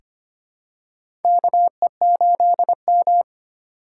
Morse code, heard as a keyed tone, about 25 wpm, 700 Hz. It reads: KE8M
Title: K1USN SST QSO Practice - 25wpm